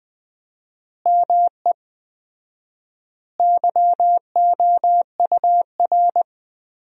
Morse code, heard as a keyed tone, about 21 wpm, 700 Hz. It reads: ME YOUR
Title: 4x Word-Spacing - Sentences from Top 200 Words - 20wpm